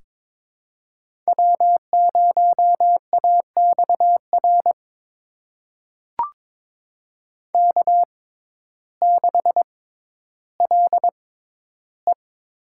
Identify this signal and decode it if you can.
Morse code, heard as a keyed tone, about 21 wpm, spaced as for 12 wpm, 700 Hz. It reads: W0AXR K6LE
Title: USA Call Signs - 6x Character Spacing - 22wpm